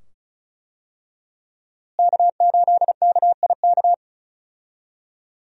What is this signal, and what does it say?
Morse code, heard as a keyed tone, about 36 wpm, 700 Hz. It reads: K8KIK
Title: USA Call Signs - 5 Characters - Repeated 2x - 35wpm